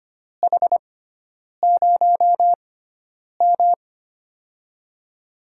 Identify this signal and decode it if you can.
Morse code, heard as a keyed tone, about 25 wpm, spaced as for 9 wpm, 700 Hz. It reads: H0M
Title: USA Call Signs - 4 Characters - 6x Character Spacing - 25wpm